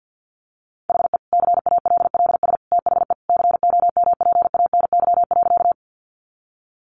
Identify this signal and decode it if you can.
Morse code, heard as a keyed tone, about 50 wpm, 700 Hz. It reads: HE CALLS THE COMPANY.
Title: Additional Easy Phrases 50wpm